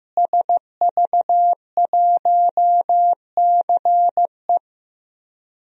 Morse code, heard as a keyed tone, about 15 wpm, 700 Hz. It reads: SV1CE
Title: DXCC-100 Call Signs - 5 Characters - Repeated 2x - 15wpm